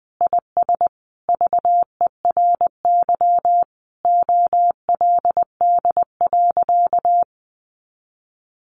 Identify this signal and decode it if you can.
Morse code, heard as a keyed tone, about 19 wpm, 700 Hz. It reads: IS VERY OLD.
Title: Sentences from Top 1100 Words 20wpm